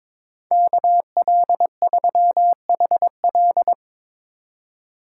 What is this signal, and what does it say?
Morse code, heard as a keyed tone, about 24 wpm, 700 Hz. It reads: KL3HL
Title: Speed Racing - USA 2x2 Call Signs 22wpm